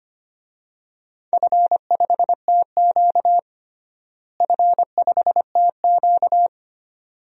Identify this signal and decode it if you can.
Morse code, heard as a keyed tone, about 25 wpm, 700 Hz. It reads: F5TQ F5TQ
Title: DXCC-100 Call Signs - 4 Characters - Repeated 2x - 25wpm